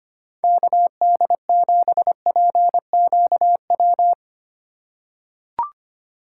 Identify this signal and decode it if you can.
Morse code, heard as a keyed tone, about 25 wpm, 700 Hz. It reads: KD7PQW
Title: USA Call Signs - 6 Characters - 3x Character Spacing - 25wpm